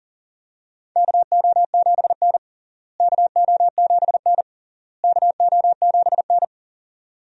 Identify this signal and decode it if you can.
Morse code, heard as a keyed tone, about 40 wpm, 700 Hz. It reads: KO7N KO7N KO7N
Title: USA Call Signs - 4 Characters - Repeated 3x - 40wpm